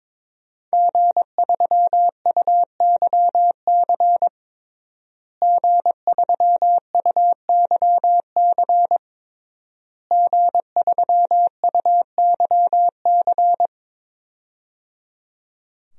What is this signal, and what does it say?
Morse code, heard as a keyed tone, about 22 wpm, 700 Hz. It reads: G3UYC G3UYC G3UYC
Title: DXCC-100 Call Signs - 5 Characters - Repeated 3x - 22wpm